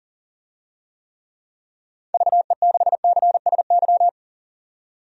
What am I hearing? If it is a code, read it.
Morse code, heard as a keyed tone, about 40 wpm, 700 Hz. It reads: VE6CSY
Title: DXCC-100 Call Signs - 6 Characters - 40wpm